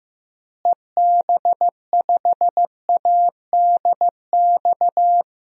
Morse code, heard as a keyed tone, about 15 wpm, 700 Hz. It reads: EB5ADX
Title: DXCC-100 Call Signs - 6 Characters - Repeated 2x - 15wpm